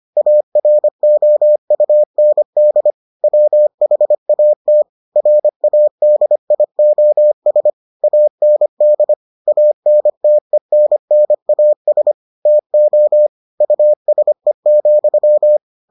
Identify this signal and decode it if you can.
Morse code, heard as a keyed tone, about 25 wpm, 600 Hz. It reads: AROUND WHAT RADIOS AND ANTENNAS TO USE,